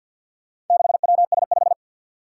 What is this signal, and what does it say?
Morse code, heard as a keyed tone, about 52 wpm, 700 Hz. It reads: 6WS5
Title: Four Challenging Letter-Number Combinations 50wpm